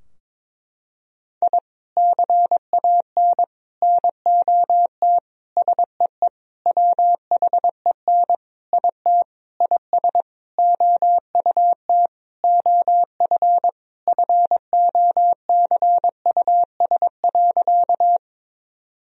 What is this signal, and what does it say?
Morse code, heard as a keyed tone, about 22 wpm, 700 Hz. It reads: I CAN NOT SEE WHEN IT IS OUT OF FOCUS.
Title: Sentences from Top 1000 Words - Review Part 4 - 22wpm